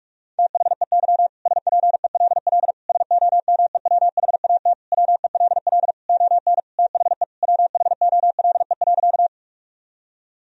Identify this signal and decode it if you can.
Morse code, heard as a keyed tone, about 46 wpm, 700 Hz. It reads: THEY SPELL SOMEWHAT WELL ON THE WHOLE.
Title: Sentences from Top 1000 Words - Review Part 1 - 45wpm